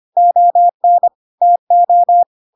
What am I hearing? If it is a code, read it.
Morse code, heard as a keyed tone, about 25 wpm, 700 Hz. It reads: ON TO
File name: Pinocchio / Pinocchio ch2 25 25 ik1hge